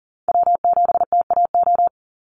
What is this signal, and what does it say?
Morse code, heard as a keyed tone, about 38 wpm, 700 Hz. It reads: W7TAO
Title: NA SOTA Chaser Callsigns 40wpm